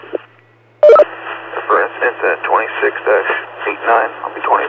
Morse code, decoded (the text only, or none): none